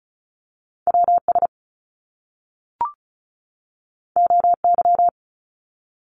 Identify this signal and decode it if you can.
Morse code, heard as a keyed tone, about 36 wpm, 700 Hz. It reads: WS OY
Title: Top Two Letter Combinations in Words 35wpm